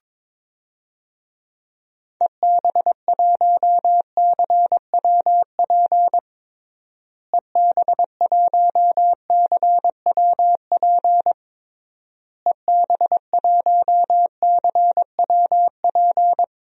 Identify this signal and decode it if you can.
Morse code, heard as a keyed tone, about 22 wpm, 700 Hz. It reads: EB1CWP EB1CWP EB1CWP
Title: DXCC-100 Call Signs - 6 Characters - Repeated 3x - 22wpm